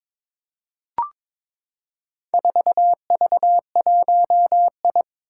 Morse code, heard as a keyed tone, about 22 wpm, 700 Hz. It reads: 4V1I
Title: Four Challenging Letter-Number Combinations 22wpm